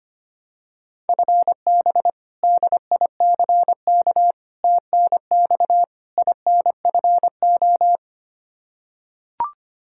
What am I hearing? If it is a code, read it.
Morse code, heard as a keyed tone, about 25 wpm, 700 Hz. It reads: FB DICK TNX INFO
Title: Practice QSO - Exchange 3 - More Info - 25wpm